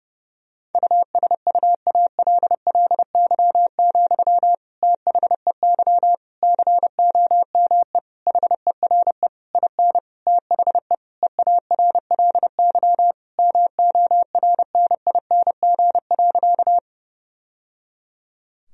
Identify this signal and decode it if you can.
Morse code, heard as a keyed tone, about 30 wpm, 700 Hz. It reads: USUALLY, THEY COME HERE IN THE EARLY MORNING.
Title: Sentences from Top 1000 Words - Review Part 3 - 30wpm